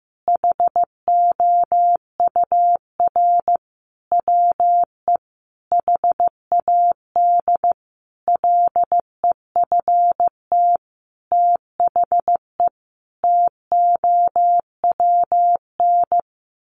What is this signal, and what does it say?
Morse code, heard as a keyed tone, about 14 wpm, 700 Hz. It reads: HOUR WE HAD LEFT THE TOWN
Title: Sentences from Top 1300 Words 15wpm